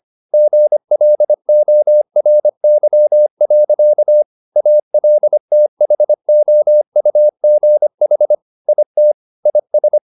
Morse code, heard as a keyed tone, about 25 wpm, 600 Hz. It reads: GLORY. ALTHOUGH IT IS
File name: the old man and the sea part 1 25wpm